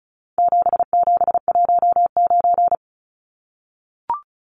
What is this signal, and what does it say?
Morse code, heard as a keyed tone, about 34 wpm, 700 Hz. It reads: 7719
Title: Numbers - 4-Digit 35wpm